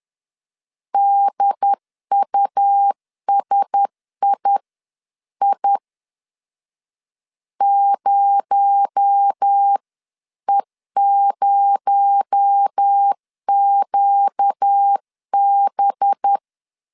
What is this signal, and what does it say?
Morse code, heard as a keyed tone, about 10 wpm, spaced as for 9 wpm, 800 Hz. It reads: DUSII 0E0QB
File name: CW / DET / 1 6 1500C S C 10wpm 4252